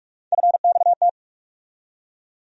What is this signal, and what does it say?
Morse code, heard as a keyed tone, about 43 wpm, 700 Hz. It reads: FXT
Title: USA Call Sign Postfixes - 3 Letters 45wpm